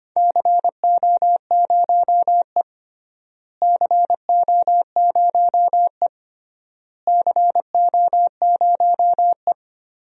Morse code, heard as a keyed tone, about 25 wpm, 700 Hz. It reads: CO0E CO0E CO0E
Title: DXCC-100 Call Signs - 4 Characters - Repeated 3x - 25wpm